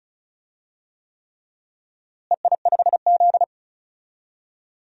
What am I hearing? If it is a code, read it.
Morse code, heard as a keyed tone, about 34 wpm, 700 Hz. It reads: EI5Z
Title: DXCC-100 Call Signs - 4 Characters - Repeated 3x - 35wpm